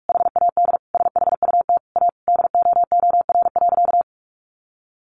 Morse code, heard as a keyed tone, about 44 wpm, 700 Hz. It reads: HAD SHUT A DOOR.
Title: A Sound Of Thunder 45wpm